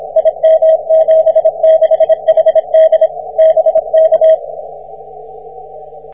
Morse code, heard as a keyed tone, about 26 wpm, 640 Hz. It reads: IM76HD BK
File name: wp / zb2eo 6m cw